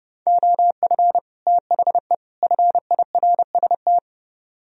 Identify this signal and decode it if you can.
Morse code, heard as a keyed tone, about 40 wpm, spaced as for 35 wpm, 700 Hz. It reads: OF THE FIRST
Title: Speed-Racing - Sets of 3 Words 30wpm